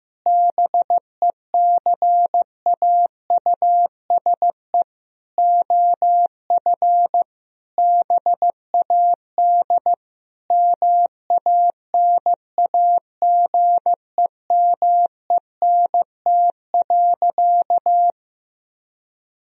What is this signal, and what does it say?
Morse code, heard as a keyed tone, about 15 wpm, 700 Hz. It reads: BECAUSE OF BAD MANAGEMENT.
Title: Sentences from Top 1000 Words - Review Part 4 - 15wpm